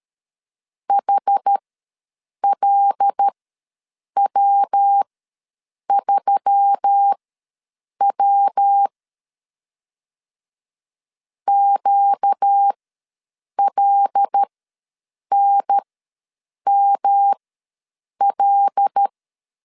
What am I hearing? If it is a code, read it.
Morse code, heard as a keyed tone, about 13 wpm, spaced as for 7 wpm, 800 Hz. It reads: HLW3W QLNML
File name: LEZIONI / 5 LBVC / 5 L ws 12wpm 2958